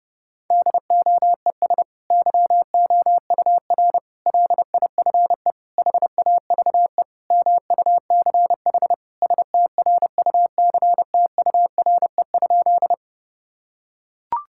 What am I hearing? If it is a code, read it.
Morse code, heard as a keyed tone, about 32 wpm, 700 Hz. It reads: DOES YOUR LIFE HAVE MUCH STRUCTURE?
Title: Sentences from Top 1000 Words - Review Part 6 - 30wpm